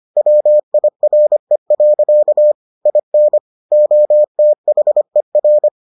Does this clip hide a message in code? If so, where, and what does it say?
Yes: Morse code from 0.2 to 5.7 s, heard as a keyed tone, about 25 wpm, 600 Hz. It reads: WIRE. IN OTHER